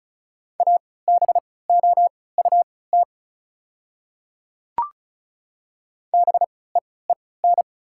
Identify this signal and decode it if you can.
Morse code, heard as a keyed tone, about 36 wpm, spaced as for 20 wpm, 690 Hz. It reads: ABOUT BEEN
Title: 3x Character Spacing - Top 100 Words - 35wpm